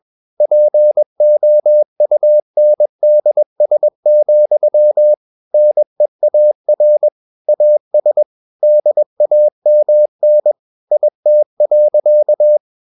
Morse code, heard as a keyed tone, about 21 wpm, 600 Hz. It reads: POUNDS, NEAR AS DAMN IT.
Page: CW med Gnister